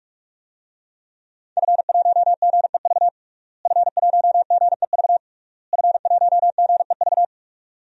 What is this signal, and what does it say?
Morse code, heard as a keyed tone, about 46 wpm, 700 Hz. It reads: F1GEV F1GEV F1GEV
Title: DXCC-100 Call Signs - 5 Characters - Repeated 3x - 45wpm